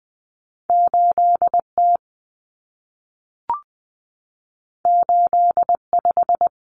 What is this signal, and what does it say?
Morse code, heard as a keyed tone, about 20 wpm, 700 Hz. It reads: 8T 85
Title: Common Cut Numbers - 2-Digit 20wpm